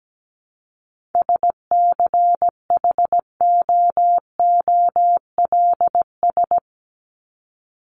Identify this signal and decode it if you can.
Morse code, heard as a keyed tone, about 17 wpm, 700 Hz. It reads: SCHOOLS
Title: ES:S Suffix - Top 100 Words with Base Words 17wpm